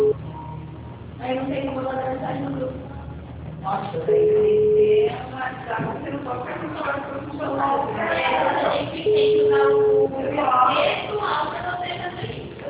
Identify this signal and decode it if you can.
Ringback tone (Europe, 425 Hz), with speech.